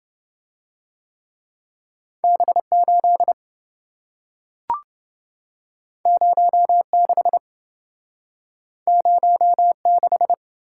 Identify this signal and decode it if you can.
Morse code, heard as a keyed tone, about 27 wpm, 700 Hz. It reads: B8 06 06